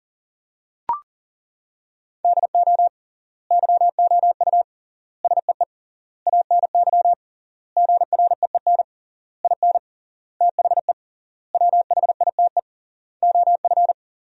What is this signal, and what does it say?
Morse code, heard as a keyed tone, about 40 wpm, 700 Hz. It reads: DO YOU SEE ANY GREEN IN THE WHITE OF
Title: 3x Word-Spacing - Sentences from Top 500 Words - 40wpm